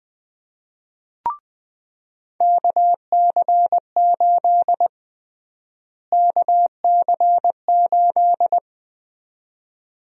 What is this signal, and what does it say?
Morse code, heard as a keyed tone, about 20 wpm, 700 Hz. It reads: KC8 KC8
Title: USA Call Sign Prefixes - Repeated 2x - 20wpm